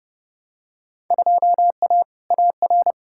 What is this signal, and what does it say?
Morse code, heard as a keyed tone, about 30 wpm, 700 Hz. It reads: 2A AR
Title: Field Day Exchanges - 30wpm